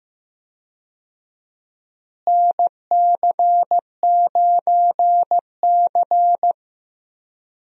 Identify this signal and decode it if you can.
Morse code, heard as a keyed tone, about 15 wpm, 700 Hz. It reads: NC9C
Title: USA Call Signs - 4 Characters - Repeated 3x - 15wpm